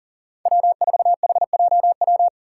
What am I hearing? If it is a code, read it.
Morse code, heard as a keyed tone, about 39 wpm, 700 Hz. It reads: W4HJW
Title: Five Challenging Letter-Number Combinations 40wpm